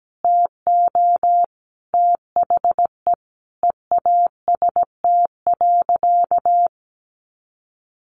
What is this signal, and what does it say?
Morse code, heard as a keyed tone, about 17 wpm, 700 Hz. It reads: TO THE EAST.
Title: Sentences from Top 1300 Words 17wpm